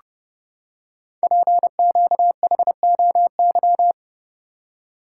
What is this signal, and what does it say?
Morse code, heard as a keyed tone, about 28 wpm, 700 Hz. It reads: PQHOY
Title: Code Groups - 5 Letters - 30wpm